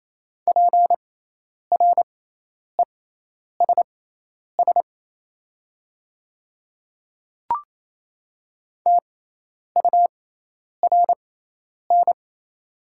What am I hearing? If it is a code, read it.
Morse code, heard as a keyed tone, about 27 wpm, spaced as for 10 wpm, 700 Hz. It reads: PRESS TURN
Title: 6x Character Spacing - Top 300 Words - 28wpm